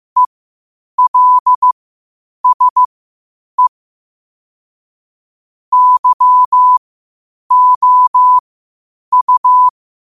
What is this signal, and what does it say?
Morse code, heard as a keyed tone, about 15 wpm, spaced as for 8 wpm, 1000 Hz. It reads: ELSE YOU
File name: qotdfiles / 8wpm